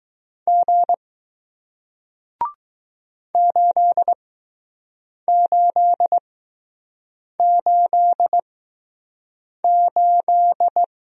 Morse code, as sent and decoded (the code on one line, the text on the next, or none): --. / ---.. ---.. ---.. ---..
G 8888